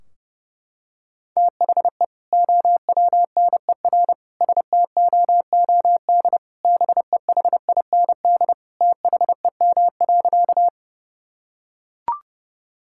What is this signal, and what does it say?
Morse code, heard as a keyed tone, about 30 wpm, 700 Hz. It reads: THE OWNER STOOD BEHIND THEM.
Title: Sentences from Top 1600 Words 30wpm